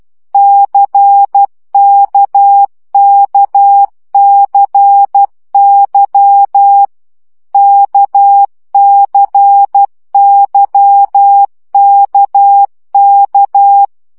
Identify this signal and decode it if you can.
Morse code, heard as a keyed tone, about 12 wpm, 800 Hz. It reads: CKKCY KCYKK
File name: CW gyakorlas k c y 10 baud